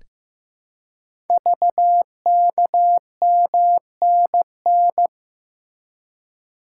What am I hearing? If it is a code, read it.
Morse code, heard as a keyed tone, about 15 wpm, 700 Hz. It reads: VKMNN
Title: Code Groups - 5 Letters - 15wpm